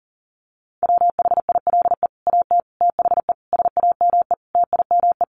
Morse code, heard as a keyed tone, about 40 wpm, 700 Hz. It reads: WHILE AT THE SAME TIME
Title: Sets of 5 Words 40wpm